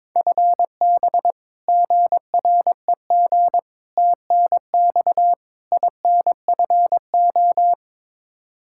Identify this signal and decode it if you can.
Morse code, heard as a keyed tone, about 21 wpm, 700 Hz. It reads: FB GREG TNX INFO
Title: Practice QSO - Exchange 3 - More Info - 22wpm